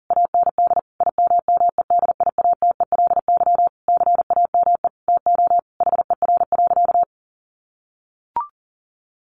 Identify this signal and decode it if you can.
Morse code, heard as a keyed tone, about 40 wpm, 700 Hz. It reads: AND IMMEDIATELY CAME TO HER.
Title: Sentences from Top 1100 Words 40wpm